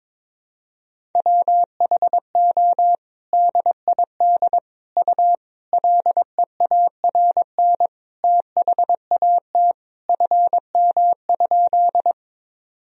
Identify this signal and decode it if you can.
Morse code, heard as a keyed tone, about 22 wpm, 700 Hz. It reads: WHO DID U LEARN THAT FM?
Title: Sentences from Top 300 Words and Abbreviations 22wpm